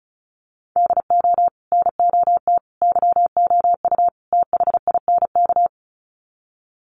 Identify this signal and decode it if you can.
Morse code, heard as a keyed tone, about 34 wpm, 700 Hz. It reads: DO NOT YOU THINK
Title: Sets of 4 Words 35wpm